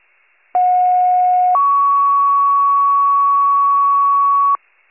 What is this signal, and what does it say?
Two-tone page, heard as a one-second tone, then a three-second tone, about 730 then 1120 Hz. It is heard from 0.5 to 4.6 s.